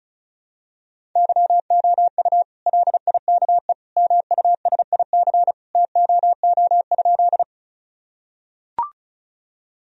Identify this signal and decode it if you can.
Morse code, heard as a keyed tone, about 34 wpm, 700 Hz. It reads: YOU LIKE MUSIC TOO?
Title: Sentences from Top 1000 Words - Review Part 1 - 35wpm